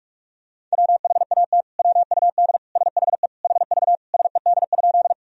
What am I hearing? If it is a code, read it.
Morse code, heard as a keyed tone, about 43 wpm, 700 Hz. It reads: WHAT WUD SHE HV SED?
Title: Sentences from Top 100 Words and Abbreviations 45wpm